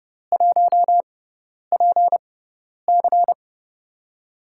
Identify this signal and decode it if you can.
Morse code, heard as a keyed tone, about 29 wpm, spaced as for 10 wpm, 700 Hz. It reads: 1PC
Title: USA Call Signs - 4 Characters - 6x Character Spacing - 30wpm